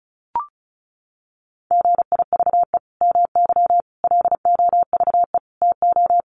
Morse code, heard as a keyed tone, about 34 wpm, 700 Hz. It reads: GIVE MY LOVE TO
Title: Sentences from Top 1400 Words 35wpm